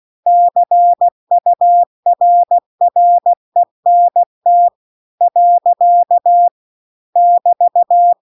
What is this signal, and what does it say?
Morse code, heard as a keyed tone, about 16 wpm, 700 Hz. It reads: CURRENT . <BT>
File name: CW-The-Big-Twoheaded-River-2-16WPM